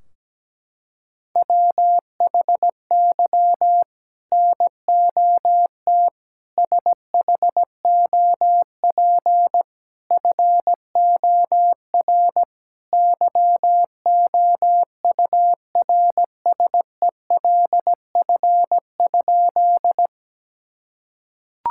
Morse code, heard as a keyed tone, about 17 wpm, 700 Hz. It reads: WHY NOT SHOP FOR YOURSELF?
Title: Sentences from Top 1600 Words 17wpm